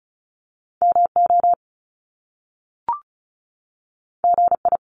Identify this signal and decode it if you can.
Morse code, heard as a keyed tone, about 36 wpm, 700 Hz. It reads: MO GI
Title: Top Two Letter Combinations in Words 35wpm